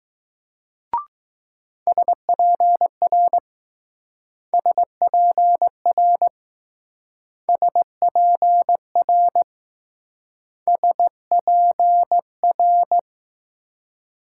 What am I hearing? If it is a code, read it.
Morse code, heard as a keyed tone, about 20 wpm, 700 Hz. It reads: SPR SPR SPR SPR